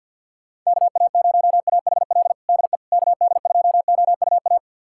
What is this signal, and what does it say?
Morse code, heard as a keyed tone, about 52 wpm, 700 Hz. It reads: KA0AHL DE KD2OUA
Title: USA Callsign DE Callsign 50wpm